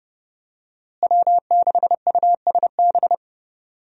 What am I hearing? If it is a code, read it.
Morse code, heard as a keyed tone, about 30 wpm, 700 Hz. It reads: W6USB
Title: USA Call Signs - 5 Characters - Repeated 2x - 30wpm